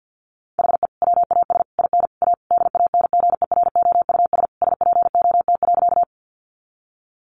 Morse code, heard as a keyed tone, about 50 wpm, 700 Hz. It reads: HE WAS IN A DANGEROUS SPOT.
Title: Sentences from Top 1200 Words 50wpm